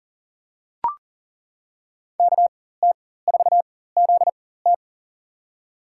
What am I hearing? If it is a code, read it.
Morse code, heard as a keyed tone, about 40 wpm, spaced as for 19 wpm, 700 Hz. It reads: KT4ZT
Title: USA Call Signs - 5 Characters - 4x Character Spacing - 40wpm